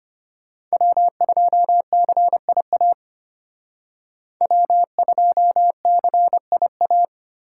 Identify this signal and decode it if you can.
Morse code, heard as a keyed tone, about 32 wpm, 700 Hz. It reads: W2CIA W2CIA